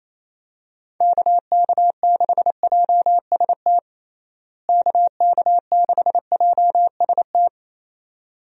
Morse code, heard as a keyed tone, about 28 wpm, 700 Hz. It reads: KK6JST KK6JST
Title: USA Call Signs - 6 Characters - Repeated 2x - 28wpm